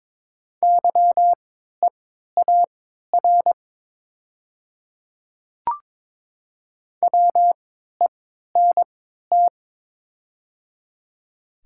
Morse code, heard as a keyed tone, about 23 wpm, spaced as for 13 wpm, 700 Hz. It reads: YEAR WENT
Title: 3x Character Spacing - Top 300 Words - 22wpm